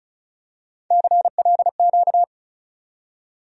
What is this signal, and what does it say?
Morse code, heard as a keyed tone, about 34 wpm, 700 Hz. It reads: CLQ
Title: USA Call Sign Postfixes - 3 Letters 35wpm